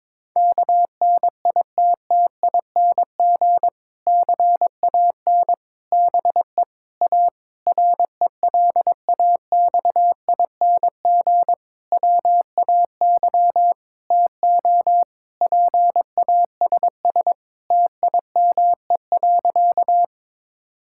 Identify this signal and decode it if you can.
Morse code, heard as a keyed tone, about 22 wpm, 700 Hz. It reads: KNITTING CAN BE A RELAXING WAY TO PASS TIME.
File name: Difficult Sentences with Hints - 22wpm